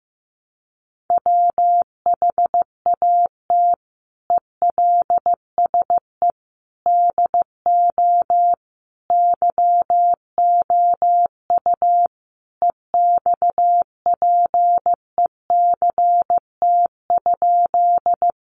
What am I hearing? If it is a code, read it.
Morse code, heard as a keyed tone, about 15 wpm, 700 Hz. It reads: WHAT ELSE DO YOU EXPECT?
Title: Sentences from Top 1100 Words 15wpm